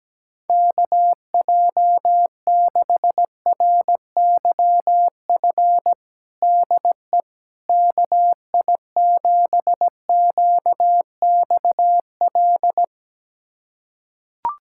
Morse code, heard as a keyed tone, about 17 wpm, 700 Hz. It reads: KJ6RYF DE KI7QXL
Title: USA Callsign DE Callsign - 6 Characters - 17wpm